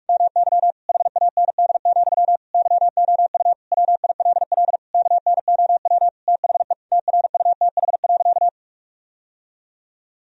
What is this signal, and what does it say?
Morse code, heard as a keyed tone, about 46 wpm, 700 Hz. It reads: MY HAND, YOU WILL KNOW THE TRUTH.
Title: Sentences from Top 1000 Words - Review Part 3 - 45wpm